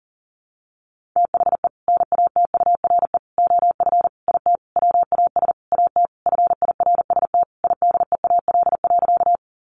Morse code, heard as a keyed tone, about 40 wpm, 700 Hz. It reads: THE NATURE OF IT WAS AT FIRST IDEAL.
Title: Sentences from Top 1400 Words 40wpm